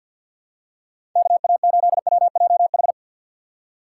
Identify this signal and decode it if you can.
Morse code, heard as a keyed tone, about 50 wpm, 700 Hz. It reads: KA8WJH
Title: USA Call Signs - 6 Characters - 6x Character Spacing - 50wpm